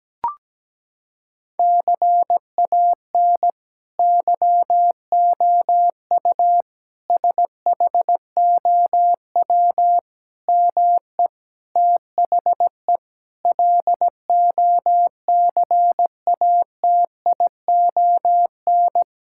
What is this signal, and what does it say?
Morse code, heard as a keyed tone, about 17 wpm, 700 Hz. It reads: CAN YOU SHOW ME THE LOCATION
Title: Sentences from Top 1000 Words - Review Part 6 - 17wpm